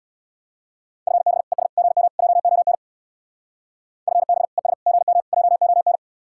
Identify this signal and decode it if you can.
Morse code, heard as a keyed tone, about 66 wpm, 700 Hz. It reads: FB ES GM JON FB ES GM JON